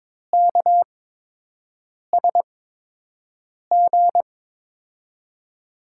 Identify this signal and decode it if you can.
Morse code, heard as a keyed tone, about 21 wpm, spaced as for 6 wpm, 700 Hz. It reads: KSG
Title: USA Call Signs - 6 Characters - 8x Character Spacing - 22wpm